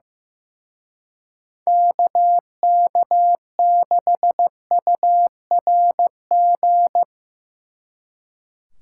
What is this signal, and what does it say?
Morse code, heard as a keyed tone, about 17 wpm, 700 Hz. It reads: KK6URG
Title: Speed Racing - USA 2x3 Call Signs 15wpm